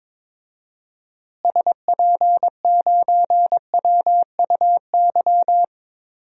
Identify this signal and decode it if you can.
Morse code, heard as a keyed tone, about 22 wpm, 700 Hz. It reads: SP9WUY
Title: DXCC-100 Call Signs - 6 Characters - Repeated 2x - 22wpm